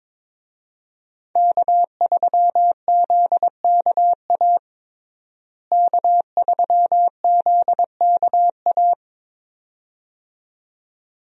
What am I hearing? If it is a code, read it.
Morse code, heard as a keyed tone, about 22 wpm, 700 Hz. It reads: K3ZKA K3ZKA
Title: USA Call Signs - 5 Characters - Repeated 2x - 22wpm